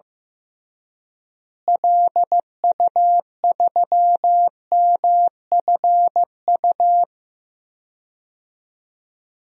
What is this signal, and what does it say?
Morse code, heard as a keyed tone, about 15 wpm, 700 Hz. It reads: LU3MFU
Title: DXCC-100 Call Signs - 6 Characters - Repeated 2x - 15wpm